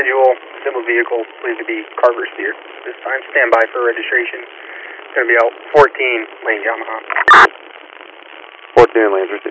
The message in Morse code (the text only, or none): none